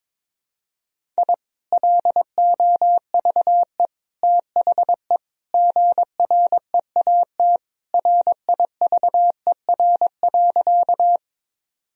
Morse code, heard as a keyed tone, about 22 wpm, 700 Hz. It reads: I LOVE THE GREAT RIVER.
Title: Sentences from Top 1000 Words - Review Part 1 - 22wpm